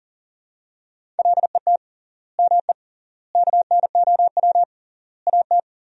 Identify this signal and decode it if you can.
Morse code, heard as a keyed tone, about 40 wpm, 700 Hz. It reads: LET ME KNOW AT
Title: 3x Word-Spacing - Sentences from Top 300 Words - 40wpm